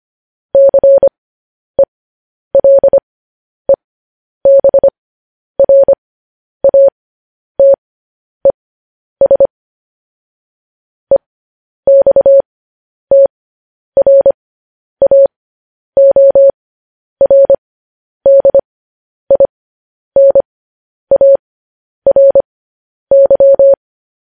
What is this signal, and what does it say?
Morse code, heard as a keyed tone, about 25 wpm, spaced as for 10 wpm, 550 Hz. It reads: CELEBRATES EXTRAORDINARY
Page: news headlines in morse code at 10 wpm